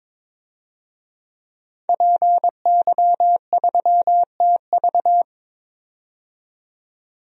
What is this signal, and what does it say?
Morse code, heard as a keyed tone, about 22 wpm, 700 Hz. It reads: PY3TV
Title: DXCC-100 Call Signs - 5 Characters - 22wpm